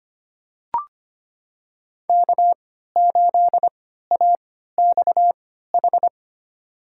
Morse code, heard as a keyed tone, about 25 wpm, spaced as for 14 wpm, 700 Hz. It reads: K8AXH
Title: USA Call Signs - 5 Characters - 3x Character Spacing - 25wpm